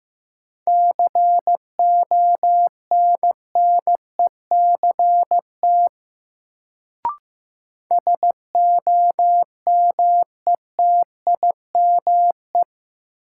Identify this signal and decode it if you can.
Morse code, heard as a keyed tone, about 15 wpm, 700 Hz. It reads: CONNECT SOMETIME
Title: Sentences from Top 1500 Words - New Word Drill Alt - 15wpm